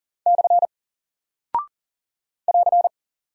Morse code, heard as a keyed tone, about 40 wpm, 700 Hz. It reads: / <AR>
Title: Single Character (Letter:Number:Punctuation:Prosign) Alt 40wpm